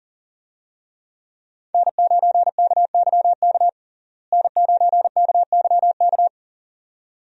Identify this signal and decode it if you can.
Morse code, heard as a keyed tone, about 40 wpm, 700 Hz. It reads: N9KYK N9KYK
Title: USA Call Signs - 5 Characters - Repeated 2x - 40wpm